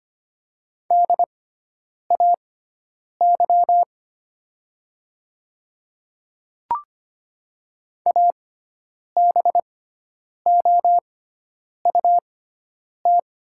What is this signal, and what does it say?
Morse code, heard as a keyed tone, about 25 wpm, spaced as for 9 wpm, 700 Hz. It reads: DAY ABOUT
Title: 6x Character Spacing - Top 500 Words - 25wpm